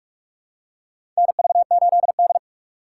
Morse code, heard as a keyed tone, about 43 wpm, 700 Hz. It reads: NV8D
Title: USA Call Signs - 2x Character Spacing - 45wpm